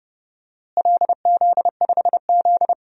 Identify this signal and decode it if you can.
Morse code, heard as a keyed tone, about 30 wpm, 700 Hz. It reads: LZ5Z
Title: DXCC-100 Call Signs - 4 Characters - Repeated 3x - 30wpm